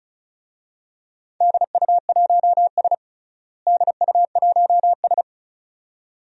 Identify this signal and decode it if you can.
Morse code, heard as a keyed tone, about 34 wpm, 700 Hz. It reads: DU1S DU1S
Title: DXCC-100 Call Signs - 4 Characters - Repeated 2x - 35wpm